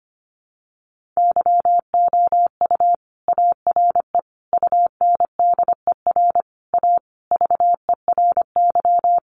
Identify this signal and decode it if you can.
Morse code, heard as a keyed tone, about 25 wpm, 700 Hz. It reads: YOU ARE UNDER A VERY
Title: Sentences from Top 1200 Words 25wpm